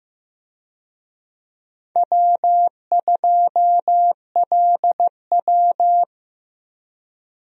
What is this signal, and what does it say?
Morse code, heard as a keyed tone, about 15 wpm, 700 Hz. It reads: W2LW
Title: USA Call Signs - 4 Characters - 15wpm